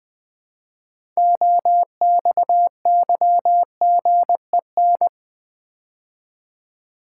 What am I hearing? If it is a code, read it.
Morse code, heard as a keyed tone, about 20 wpm, 700 Hz. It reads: OXYGEN
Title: Build A Word - 2 Letters At A Time - 6-Letter Words - 20wpm